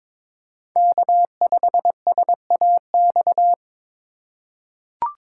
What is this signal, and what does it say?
Morse code, heard as a keyed tone, about 22 wpm, 700 Hz. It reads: K5SAX
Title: USA Call Signs - 5 Characters - 3x Character Spacing - 22wpm